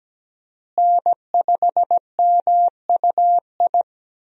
Morse code, heard as a keyed tone, about 17 wpm, 700 Hz. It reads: N5MUI
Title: USA Call Signs - 5 Characters - Repeated 2x - 17wpm